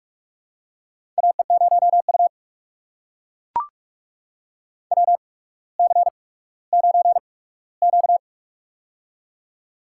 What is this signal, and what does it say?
Morse code, heard as a keyed tone, about 44 wpm, spaced as for 19 wpm, 700 Hz. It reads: AE0U WC9Q
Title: USA Call Signs - 4 Characters - 8x Character Spacing - 45wpm